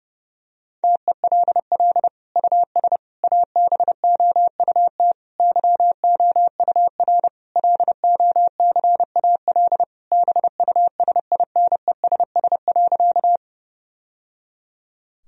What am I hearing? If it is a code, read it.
Morse code, heard as a keyed tone, about 32 wpm, 700 Hz. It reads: TELL US ABOUT YOUR LOCAL BUSINESS.
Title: Sentences from Top 1000 Words - Review Part 3 - 30wpm